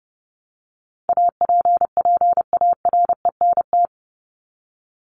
Morse code, heard as a keyed tone, about 30 wpm, 700 Hz. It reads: APPARENT
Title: LY Suffix - Top 100 Words with Base Words 30wpm